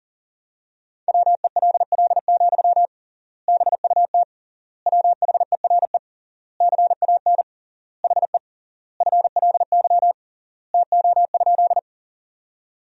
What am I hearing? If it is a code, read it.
Morse code, heard as a keyed tone, about 40 wpm, 700 Hz. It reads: WELL, BUT WHERE CAN HE FLY TO?
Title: 3x Word-Spacing - Sentences from Top 500 Words - 40wpm